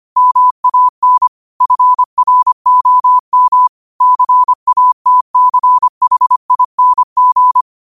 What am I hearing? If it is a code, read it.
Morse code, heard as a keyed tone, about 25 wpm, 1000 Hz. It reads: MAN FROM CATCHING
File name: qotdfiles / 25wpm